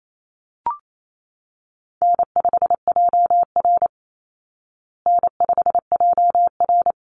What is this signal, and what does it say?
Morse code, heard as a keyed tone, about 28 wpm, 700 Hz. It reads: N5JR N5JR
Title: Easy Call Signs Repeated 2x - 28wpm